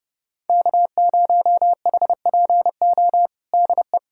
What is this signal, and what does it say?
Morse code, heard as a keyed tone, about 30 wpm, 700 Hz. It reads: K0HPO DE
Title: USA Callsign DE Callsign - 5 Characters - 30wpm